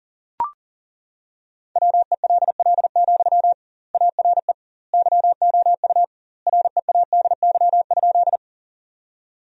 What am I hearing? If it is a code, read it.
Morse code, heard as a keyed tone, about 40 wpm, 700 Hz. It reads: WELL, ARE YOU READY?
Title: 2x Word-Spacing - Sentences from Top 400 Words - 40wpm